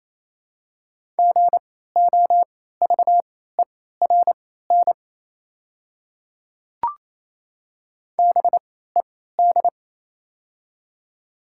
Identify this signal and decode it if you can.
Morse code, heard as a keyed tone, about 27 wpm, spaced as for 16 wpm, 700 Hz. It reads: GOVERN BED
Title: 3x Character Spacing - Top 500 Words - 28wpm